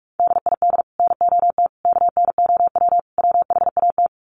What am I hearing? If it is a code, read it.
Morse code, heard as a keyed tone, about 44 wpm, 700 Hz. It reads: DID NOT KNOW WHAT
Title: Sets of 4 Words 45wpm